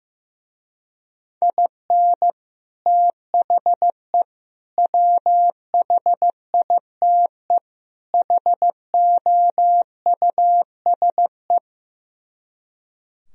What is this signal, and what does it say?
Morse code, heard as a keyed tone, about 15 wpm, 700 Hz. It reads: IN THE WHITE HOUSE
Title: Speed-Racing - Sets of 4 Words 15wpm